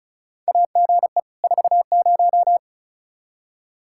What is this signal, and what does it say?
Morse code, heard as a keyed tone, about 36 wpm, 700 Hz. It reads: AGE 40
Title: Practice QSO - Exchange 3 - More Info - 35wpm